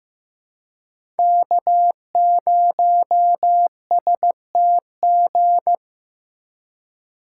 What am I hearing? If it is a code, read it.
Morse code, heard as a keyed tone, about 15 wpm, 700 Hz. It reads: K0STG
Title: USA Call Signs - 5 Characters - 3x Character Spacing - 15wpm